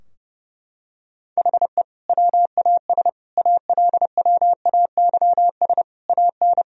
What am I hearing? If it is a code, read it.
Morse code, heard as a keyed tone, about 32 wpm, 700 Hz. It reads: HE WAS ALWAYS AN
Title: Sentences from Top 1000 Words - Review Part 2 - 30wpm